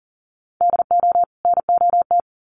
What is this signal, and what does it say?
Morse code, heard as a keyed tone, about 40 wpm, 700 Hz. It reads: DO NOT
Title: Sets of 2 Words 40wpm